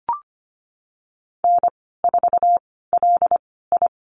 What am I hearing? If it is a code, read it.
Morse code, heard as a keyed tone, about 25 wpm, spaced as for 16 wpm, 700 Hz. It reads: N4LI
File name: Lesson 15g - Number 5 - Call Signs 25wpm